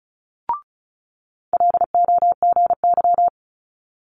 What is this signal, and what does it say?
Morse code, heard as a keyed tone, about 36 wpm, 700 Hz. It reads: LOGY
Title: Multi-letter Phonemes Quick Fire 35wpm